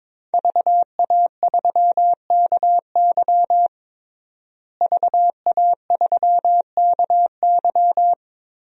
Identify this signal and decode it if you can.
Morse code, heard as a keyed tone, about 22 wpm, 700 Hz. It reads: VA3KY VA3KY
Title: DXCC-100 Call Signs - 5 Characters - Repeated 3x - 22wpm